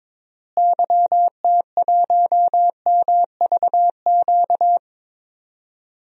Morse code, heard as a keyed tone, about 22 wpm, 700 Hz. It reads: YT1MVQ
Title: DXCC-100 Call Signs - 6 Characters - Repeated 3x - 22wpm